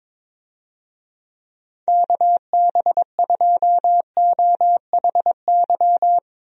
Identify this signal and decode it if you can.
Morse code, heard as a keyed tone, about 22 wpm, 700 Hz. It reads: KB2OHY
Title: USA Call Signs - 6 Characters - Repeated 3x - 22wpm